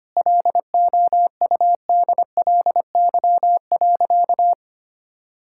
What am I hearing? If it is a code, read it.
Morse code, heard as a keyed tone, about 25 wpm, 700 Hz. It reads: LOUDLY.
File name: practice / 25wpm / Sentences with 2-Word Phrases - 25wpm